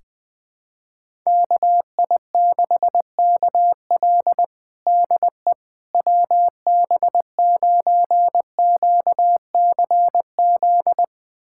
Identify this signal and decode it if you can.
Morse code, heard as a keyed tone, about 20 wpm, 690 Hz. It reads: KI6KL DE WB9QCZ
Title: USA Callsign DE Callsign 20wpm